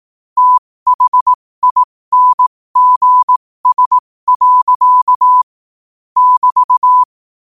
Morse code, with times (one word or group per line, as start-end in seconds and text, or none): 0.4-5.4 THINGS.
6.2-7.0 <BT>